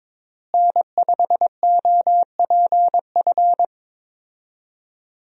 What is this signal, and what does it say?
Morse code, heard as a keyed tone, about 21 wpm, 700 Hz. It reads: N5OPF
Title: USA Call Signs - 5 Characters - Repeated 2x - 22wpm